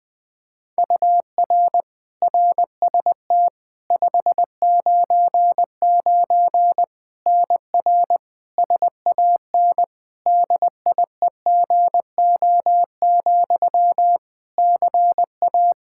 Morse code, heard as a keyed tone, about 20 wpm, 700 Hz. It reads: UR RST 599 NR SAN DIEGO, CA
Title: Practice QSO - Exchange 1 - Beginning - 20wpm